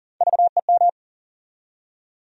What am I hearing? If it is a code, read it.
Morse code, heard as a keyed tone, about 40 wpm, 700 Hz. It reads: VEM
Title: USA Call Sign Postfixes - 3 Letters 40wpm